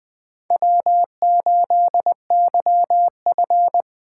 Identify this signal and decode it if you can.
Morse code, heard as a keyed tone, about 20 wpm, 700 Hz. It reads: W8YF
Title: USA Call Signs - 4 Characters - 4x Character Spacing - 20wpm